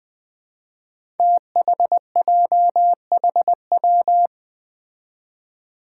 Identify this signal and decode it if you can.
Morse code, heard as a keyed tone, about 20 wpm, 700 Hz. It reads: THJHW
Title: Five Challenging Letter-Number Combinations 20wpm